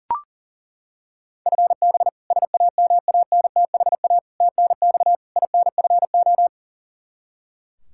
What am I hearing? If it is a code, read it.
Morse code, heard as a keyed tone, about 37 wpm, 700 Hz. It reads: FB SAMANTHA TNX INFO
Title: K1USN SST QSO Practice 40wpm